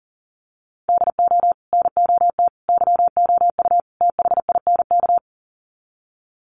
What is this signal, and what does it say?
Morse code, heard as a keyed tone, about 40 wpm, 700 Hz. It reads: DO NOT YOU THINK
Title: Sets of 4 Words 40wpm